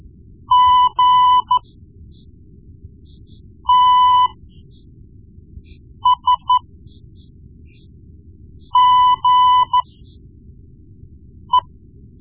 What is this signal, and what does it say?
Morse code, heard as a keyed tone, about 13 wpm, spaced as for 4 wpm, 1000 Hz. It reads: GTSGE
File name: Morse Lessons / Morse Lessons / Tape-1B